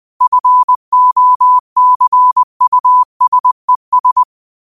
Morse code, heard as a keyed tone, about 20 wpm, 1000 Hz. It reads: FOCUSES
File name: qotdfiles / 20wpm